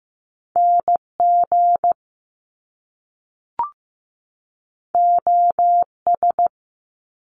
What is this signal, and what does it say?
Morse code, heard as a keyed tone, about 15 wpm, 700 Hz. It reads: NG OS
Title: Top Two Letter Combinations in Words 15wpm